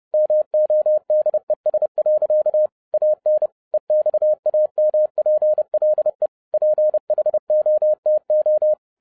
Morse code, heard as a keyed tone, about 30 wpm, 610 Hz. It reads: MODES. AN EXAMPLE PHOTO